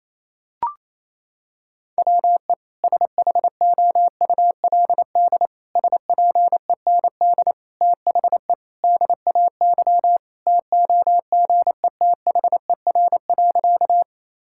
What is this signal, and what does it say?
Morse code, heard as a keyed tone, about 28 wpm, 700 Hz. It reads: WE SHOULD SPEND THE DAY TOGETHER.
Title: Sentences from Top 1000 Words - Review Part 5 - 28wpm